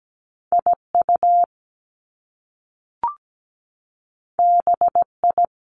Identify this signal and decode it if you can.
Morse code, heard as a keyed tone, about 17 wpm, 700 Hz. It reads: IU BI
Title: Top Two Letter Combinations in Words 17wpm